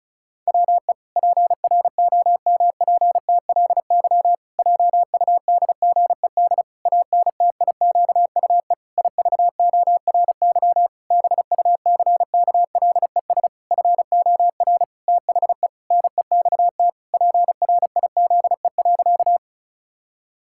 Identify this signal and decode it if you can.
Morse code, heard as a keyed tone, about 34 wpm, 700 Hz. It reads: WE PROMPTLY JUDGED ANTIQUE IVORY BUCKLES FOR THE NEXT PRIZE.
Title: Pangram Challenge v2 - 35wpm